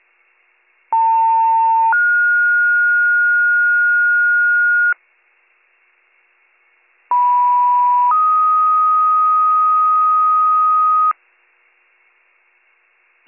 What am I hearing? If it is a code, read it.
Two-tone page, heard as a one-second tone, then a three-second tone, about 900 then 1430 Hz, about 980 then 1250 Hz. It is heard from 0.9 to 11.1 s.